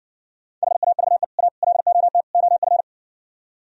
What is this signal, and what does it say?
Morse code, heard as a keyed tone, about 60 wpm, 700 Hz. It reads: HAVE A LOT OF